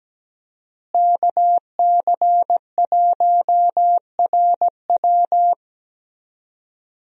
Morse code, heard as a keyed tone, about 17 wpm, 700 Hz. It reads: KC1RW
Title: USA Call Signs - 5 Characters - Repeated 2x - 17wpm